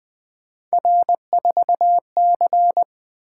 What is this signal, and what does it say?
Morse code, heard as a keyed tone, about 20 wpm, 700 Hz. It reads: R4C
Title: Three Letter-Number Combinations - 20wpm